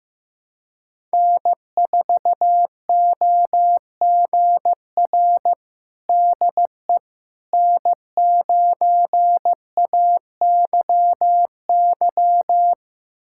Morse code, heard as a keyed tone, about 16 wpm, 700 Hz. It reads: N4OGR DE N9AYY
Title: USA Callsign DE Callsign - 5 Characters - 15wpm